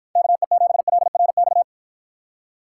Morse code, heard as a keyed tone, about 49 wpm, 700 Hz. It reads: KE7LRX